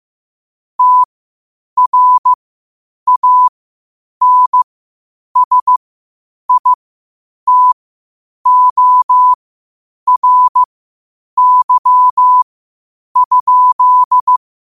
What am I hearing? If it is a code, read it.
Morse code, heard as a keyed tone, about 15 wpm, spaced as for 8 wpm, 1000 Hz. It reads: TRANSITORY?
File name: qotdfiles / 8wpm